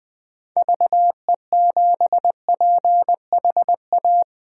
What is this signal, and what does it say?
Morse code, heard as a keyed tone, about 20 wpm, 700 Hz. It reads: VE7PHA
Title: K1USN SST QSO Practice - 20wpm